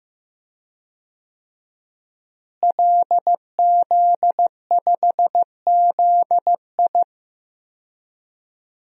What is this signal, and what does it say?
Morse code, heard as a keyed tone, about 16 wpm, 700 Hz. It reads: LZ5ZI
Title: DXCC-100 Call Signs - 5 Characters - Repeated 2x - 15wpm